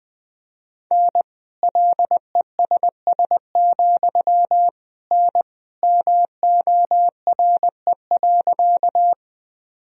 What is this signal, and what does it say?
Morse code, heard as a keyed tone, about 20 wpm, 700 Hz. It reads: N LESS, N MORE.
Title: Sentences from Top 400 Words and Abbreviations 20wpm